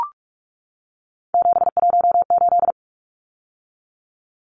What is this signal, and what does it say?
Morse code, heard as a keyed tone, about 44 wpm, 700 Hz. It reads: 718
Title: Numbers - 3-Digit 45wpm